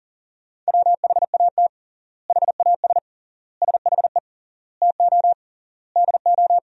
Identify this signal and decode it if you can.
Morse code, heard as a keyed tone, about 40 wpm, 700 Hz. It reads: WHAT HAS SHE TO DO
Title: 3x Word-Spacing - Sentences from Top 100 Words - 40wpm